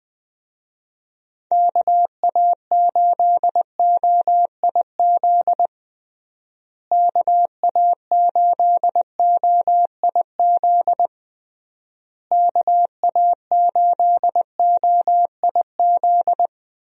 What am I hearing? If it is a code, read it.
Morse code, heard as a keyed tone, about 20 wpm, 700 Hz. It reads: KA8OIZ KA8OIZ KA8OIZ
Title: USA Call Signs - 6 Characters - Repeated 3x - 20wpm